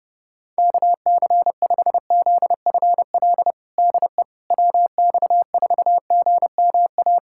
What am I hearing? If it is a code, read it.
Morse code, heard as a keyed tone, about 32 wpm, 700 Hz. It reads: KC5ZFL DE WX4GMA
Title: USA Callsign DE Callsign - 6 Characters - 30wpm